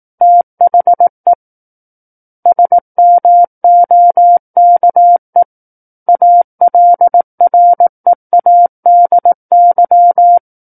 Morse code, heard as a keyed tone, about 18 wpm, 700 Hz. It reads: THE SMOKE ALREADY
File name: Book One / 09-Chapter 9 - 18 WPM